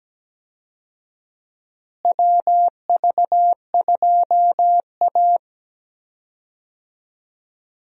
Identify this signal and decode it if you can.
Morse code, heard as a keyed tone, about 17 wpm, 700 Hz. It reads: WV2A
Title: USA Call Signs - 4 Characters - Repeated 2x - 17wpm